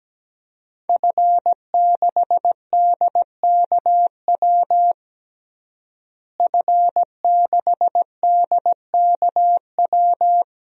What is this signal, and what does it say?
Morse code, heard as a keyed tone, about 17 wpm, 700 Hz. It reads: F6DKW F6DKW
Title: DXCC-100 Call Signs - 5 Characters - Repeated 3x - 17wpm